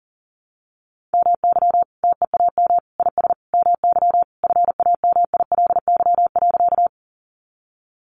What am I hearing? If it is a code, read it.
Morse code, heard as a keyed tone, about 37 wpm, 700 Hz. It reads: MY TEAM IS MY FAMILY.
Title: Sentences from Top 1500 Words 40wpm